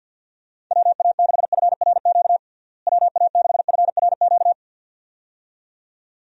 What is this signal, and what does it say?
Morse code, heard as a keyed tone, about 51 wpm, 700 Hz. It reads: WA6FRQ WA6FRQ
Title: USA Call Signs - 6 Characters - Repeated 2x - 50wpm